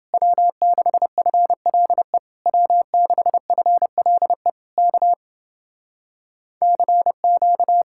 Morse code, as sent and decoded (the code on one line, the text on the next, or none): .-- -.... ..-. .-.. / .-- -.... ..-. .-.. / -.- / -.-. --.-
W6FLE W6FLE K CQ